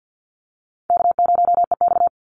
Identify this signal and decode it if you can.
Morse code, heard as a keyed tone, about 50 wpm, 700 Hz. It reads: K0EX
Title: Call Signs - 50wpm